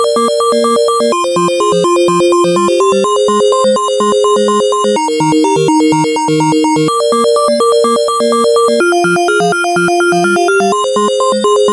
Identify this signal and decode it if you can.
There is no code.